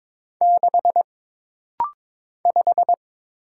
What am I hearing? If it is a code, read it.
Morse code, heard as a keyed tone, about 22 wpm, 700 Hz. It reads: 6 5
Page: This rapid-fire timing is geared to push your ICR (Instant Character Recognition) skills.